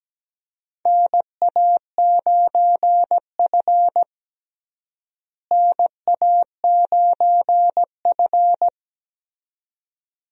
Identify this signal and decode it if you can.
Morse code, heard as a keyed tone, about 17 wpm, 700 Hz. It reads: NA9F NA9F
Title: USA Call Signs - 4 Characters - Repeated 2x - 17wpm